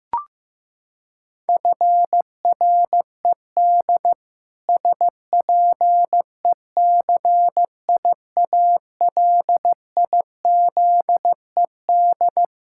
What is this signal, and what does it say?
Morse code, heard as a keyed tone, about 15 wpm, 700 Hz. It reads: FRED SPECIALIZED
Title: Pangram Challenge v2 - 15wpm